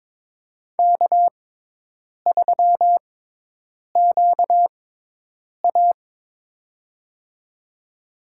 Morse code, heard as a keyed tone, about 22 wpm, spaced as for 8 wpm, 700 Hz. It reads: K3QA
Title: USA Call Signs - 4 Characters - 6x Character Spacing - 22wpm